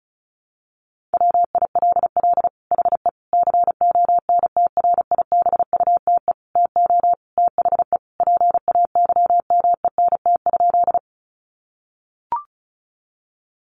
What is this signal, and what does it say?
Morse code, heard as a keyed tone, about 34 wpm, 700 Hz. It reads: WILL HE CONTRIBUTE TO THE PAYMENT?
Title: Sentences from Top 1400 Words 35wpm